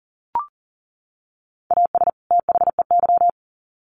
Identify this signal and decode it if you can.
Morse code, heard as a keyed tone, about 40 wpm, 700 Hz. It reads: AS THEY
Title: Sets of 2 Words 40wpm